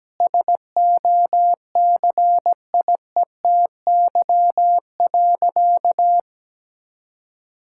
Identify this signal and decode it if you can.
Morse code, heard as a keyed tone, about 17 wpm, 700 Hz. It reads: SOCIETY.
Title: Sentences from Top 1000 Words - Review Part 6 - 17wpm